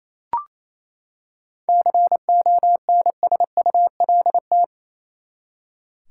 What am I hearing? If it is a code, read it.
Morse code, heard as a keyed tone, about 27 wpm, 700 Hz. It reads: CONSULT
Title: Seven Letter Words - 28wpm